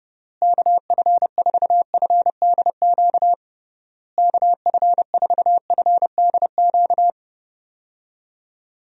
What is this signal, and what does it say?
Morse code, heard as a keyed tone, about 31 wpm, 700 Hz. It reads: KF4FDQ KF4FDQ
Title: USA Call Signs - 6 Characters - Repeated 3x - 30wpm